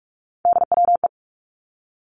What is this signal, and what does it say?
Morse code, heard as a keyed tone, about 46 wpm, 700 Hz. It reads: DWE
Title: Three Challenging Letter-Number Combinations 45wpm